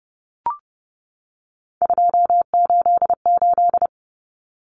Morse code, heard as a keyed tone, about 30 wpm, 700 Hz. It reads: 288
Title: Common Cut Numbers - 3-Digit 30wpm